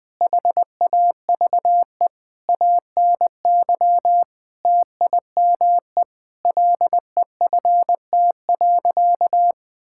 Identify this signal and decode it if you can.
Morse code, heard as a keyed tone, about 20 wpm, 700 Hz. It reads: HAVE ANY TIME LEFT.
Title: Sentences with 4-Word Phrases - Spoken Beginning - 20wpm